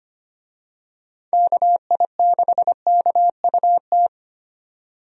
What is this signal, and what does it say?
Morse code, heard as a keyed tone, about 25 wpm, 700 Hz. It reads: KI6KUT
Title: USA Call Signs - 6 Characters - 25wpm